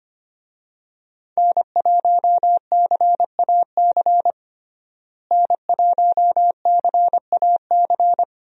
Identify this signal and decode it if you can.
Morse code, heard as a keyed tone, about 25 wpm, 700 Hz. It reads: N1CAC N1CAC
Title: USA Call Signs - 5 Characters - Repeated 2x - 25wpm